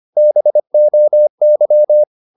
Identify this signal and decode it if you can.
Morse code, heard as a keyed tone, about 25 wpm, 600 Hz. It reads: BOY
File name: the old man and the sea part 1 25wpm